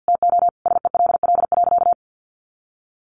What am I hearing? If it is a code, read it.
Morse code, heard as a keyed tone, about 50 wpm, 700 Hz. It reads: TO SELL.
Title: Sentences from Top 1100 Words 50wpm